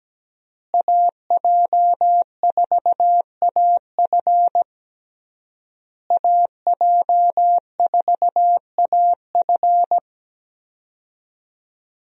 Morse code, heard as a keyed tone, about 17 wpm, 700 Hz. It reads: AJ4AF AJ4AF
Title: USA Call Signs - 5 Characters - Repeated 3x - 17wpm